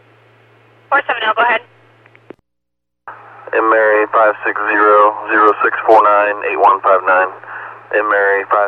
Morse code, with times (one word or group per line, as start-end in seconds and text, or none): none